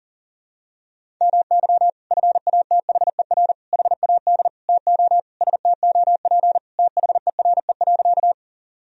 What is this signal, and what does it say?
Morse code, heard as a keyed tone, about 37 wpm, 700 Hz. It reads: MY FATHER HAD TO STOP THERE.
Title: Sentences from Top 1000 Words - Review Part 1 - 40wpm